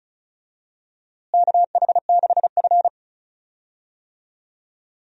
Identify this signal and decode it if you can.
Morse code, heard as a keyed tone, about 36 wpm, 700 Hz. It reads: KH6F
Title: DXCC-100 Call Signs - 4 Characters - Repeated 2x - 35wpm